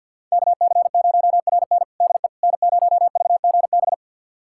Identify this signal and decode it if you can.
Morse code, heard as a keyed tone, about 46 wpm, 700 Hz. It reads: KC0RN DE N0VGB
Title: USA Callsign DE Callsign 50wpm